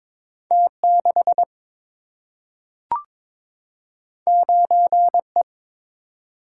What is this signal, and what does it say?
Morse code, heard as a keyed tone, about 22 wpm, 700 Hz. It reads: T6 9E
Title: Single-Letter Single-Number 22wpm